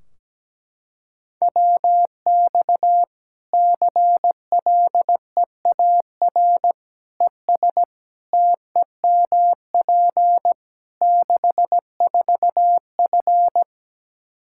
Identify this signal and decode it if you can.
Morse code, heard as a keyed tone, about 17 wpm, 700 Hz. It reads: WX CLEAR ES TEMP 64F
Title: K1USN SST QSO Practice 17wpm